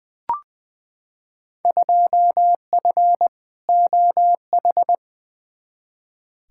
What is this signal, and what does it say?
Morse code, heard as a keyed tone, about 20 wpm, 700 Hz. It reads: 2F OH
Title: Field Day Exchanges - 20wpm